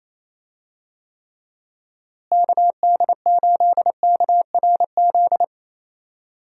Morse code, heard as a keyed tone, about 28 wpm, 700 Hz. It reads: KD8KRZ
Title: USA Call Signs - 6 Characters - Repeated 2x - 28wpm